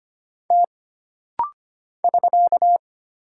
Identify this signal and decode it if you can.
Morse code, heard as a keyed tone, about 25 wpm, 700 Hz. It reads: T <SK>
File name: Single Character (Letter:Number:Punctuation:Prosign) - Rapid-Fire - 25wpm